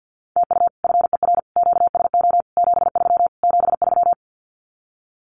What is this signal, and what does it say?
Morse code, heard as a keyed tone, about 50 wpm, 700 Hz. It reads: TU FER QSO 73 73
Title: Common QSO Phrases Encore 50wpm